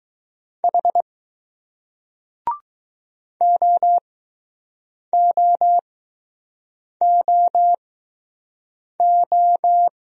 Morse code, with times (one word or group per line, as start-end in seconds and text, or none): 0.6-1.0 H
3.4-9.9 OOOO